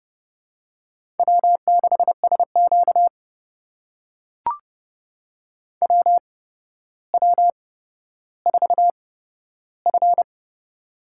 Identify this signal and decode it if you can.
Morse code, heard as a keyed tone, about 29 wpm, spaced as for 13 wpm, 700 Hz. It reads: W6SQ WW4F
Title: USA Call Signs - 4 Characters - 8x Character Spacing - 30wpm